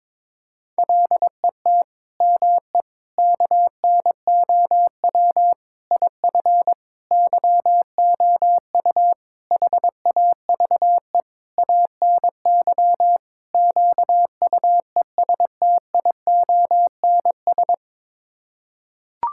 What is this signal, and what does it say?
Morse code, heard as a keyed tone, about 22 wpm, 700 Hz. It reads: LET ME KNOW IF YOU HAVE ANY QUESTIONS
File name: practice / 22wpm / Sets of 8 Words - 22wpm